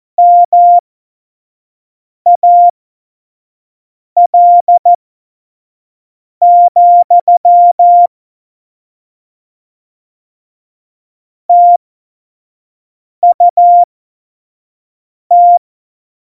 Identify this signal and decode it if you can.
Morse code, heard as a keyed tone, about 14 wpm, spaced as for 5 wpm, 700 Hz. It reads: MAL, TUT
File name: Morsekurs / Witze / Witz 19. Dezember 2017, 25 F70